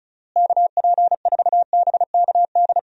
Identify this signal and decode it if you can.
Morse code, heard as a keyed tone, about 36 wpm, 700 Hz. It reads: KP4BKD
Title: DXCC-100 Call Signs - 6 Characters - Repeated 3x - 35wpm